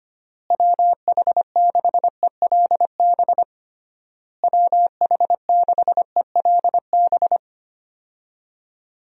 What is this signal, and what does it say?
Morse code, heard as a keyed tone, about 25 wpm, 700 Hz. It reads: WH6ELB WH6ELB
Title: USA Call Signs - 6 Characters - Repeated 3x - 25wpm